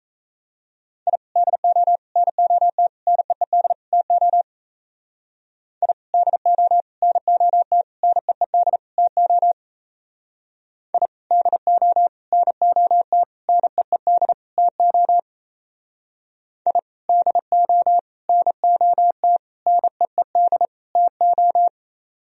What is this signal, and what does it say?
Morse code, heard as a keyed tone, about 33 wpm, 700 Hz. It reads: I DO NOT NEED TO I DO NOT NEED TO I DO NOT NEED TO I DO NOT NEED TO
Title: Speed-Racing - Sets of 5 Words 28wpm